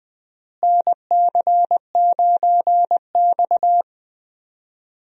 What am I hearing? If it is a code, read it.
Morse code, heard as a keyed tone, about 21 wpm, 700 Hz. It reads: NC9X
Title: USA Call Signs - 4 Characters - Repeated 3x - 20wpm